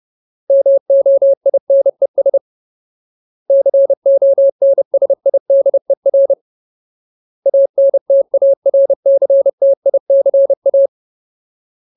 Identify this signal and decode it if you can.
Morse code, heard as a keyed tone, about 30 wpm, 550 Hz. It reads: MOINES CONSIDER ANTARCTICA
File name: Ultimate Word List / 30WPM Ultimate